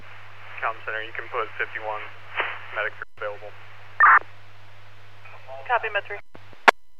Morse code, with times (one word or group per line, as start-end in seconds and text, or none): none